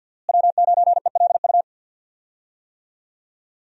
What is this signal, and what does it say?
Morse code, heard as a keyed tone, about 49 wpm, 700 Hz. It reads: W9ELU
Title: USA Call Signs - 5 Characters - Repeated 2x - 50wpm